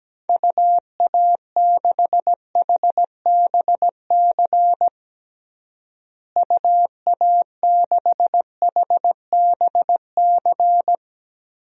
Morse code, heard as a keyed tone, about 17 wpm, 700 Hz. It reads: UA6HBC UA6HBC
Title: DXCC-100 Call Signs - 6 Characters - Repeated 2x - 17wpm